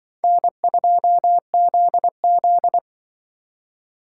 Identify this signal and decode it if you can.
Morse code, heard as a keyed tone, about 24 wpm, 700 Hz. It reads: N2ZZ
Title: Speed Racing - USA 1x2 Call Signs 20wpm